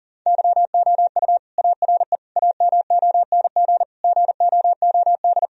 Morse code, heard as a keyed tone, about 40 wpm, 700 Hz. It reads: YOU ARE AMONG GOOD
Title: Sentences from Top 1000 Words - Review Part 3 - 40wpm